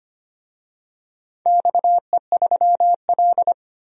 Morse code, heard as a keyed tone, about 25 wpm, 700 Hz. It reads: XE3L
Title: DXCC-100 Call Signs - 4 Characters - 25wpm